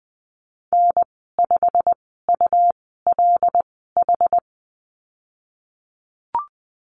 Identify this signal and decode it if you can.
Morse code, heard as a keyed tone, about 19 wpm, spaced as for 14 wpm, 700 Hz. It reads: N5ULH
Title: 213 Lesson 34g - Number 8 - Call Signs 20wpm